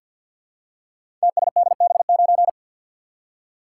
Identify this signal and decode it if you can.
Morse code, heard as a keyed tone, about 47 wpm, 700 Hz. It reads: TSDB9
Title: Code Groups - 5 Letter:Numbers - 50wpm